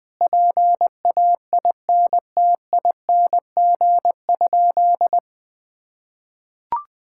Morse code, heard as a keyed tone, about 20 wpm, 700 Hz. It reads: PAINTING?
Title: Sentences from Top 1000 Words - Review Part 6 - 20wpm